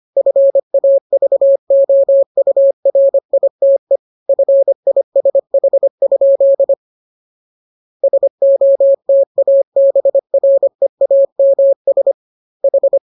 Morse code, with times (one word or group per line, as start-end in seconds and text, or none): 0.2-4.0 FAVOURITE
4.3-6.7 FISH?
8.0-9.6 SOTA
9.8-12.1 BREAMS
12.6-13.0 H